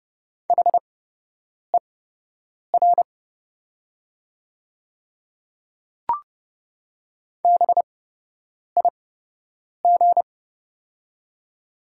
Morse code, heard as a keyed tone, about 30 wpm, spaced as for 8 wpm, 700 Hz. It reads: HER BIG
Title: 8x Character Spacing - Top 300 Words - 30wpm